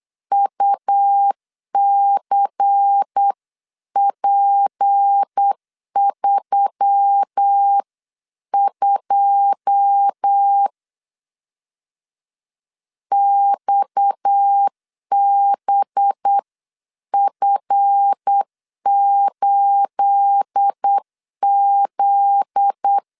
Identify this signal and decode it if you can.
Morse code, heard as a keyed tone, about 8 wpm, 800 Hz. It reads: UCP32 XBF8Z
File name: CW / DET / 1 9 1500C S B 8wpm 5624